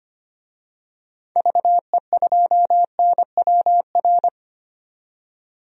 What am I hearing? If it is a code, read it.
Morse code, heard as a keyed tone, about 25 wpm, 700 Hz. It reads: VE2NWR
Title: DXCC-100 Call Signs - 6 Characters - 25wpm